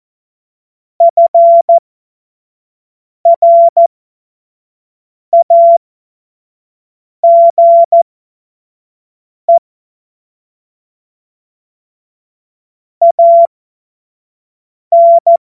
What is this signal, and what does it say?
Morse code, heard as a keyed tone, about 14 wpm, spaced as for 5 wpm, 680 Hz. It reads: FRAGE AN
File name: Morsekurs / Witze / Witz 17. Januar 2023 25 F70